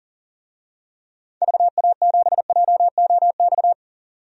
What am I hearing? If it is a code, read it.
Morse code, heard as a keyed tone, about 37 wpm, 700 Hz. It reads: VA7JOX
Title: DXCC-100 Call Signs - 6 Characters - 40wpm